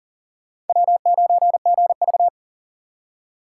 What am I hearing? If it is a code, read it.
Morse code, heard as a keyed tone, about 51 wpm, 700 Hz. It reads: W9GV